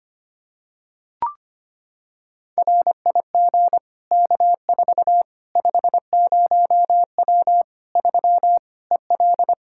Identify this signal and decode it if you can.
Morse code, heard as a keyed tone, about 25 wpm, 700 Hz. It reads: RIG K4 50W 3 EL
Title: Practice QSO - Exchange 2 - Info - 25wpm